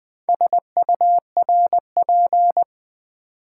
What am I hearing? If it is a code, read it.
Morse code, heard as a keyed tone, about 20 wpm, 700 Hz. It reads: SURP
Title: Build A Word - 2 Letters At A Time - 8-Letter Words - 20wpm